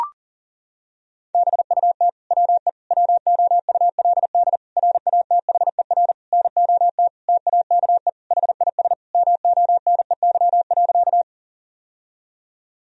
Morse code, heard as a keyed tone, about 40 wpm, 700 Hz. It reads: BUT WE WOULD RATHER NOT TAKE HIS MONEY.
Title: Sentences from Top 1000 Words - Review Part 4 - 40wpm